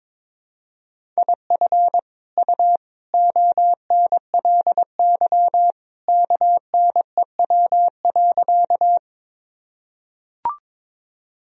Morse code, heard as a keyed tone, about 22 wpm, 700 Hz. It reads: IF U ONLY KNEW.
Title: Sentences from Top 400 Words and Abbreviations 22wpm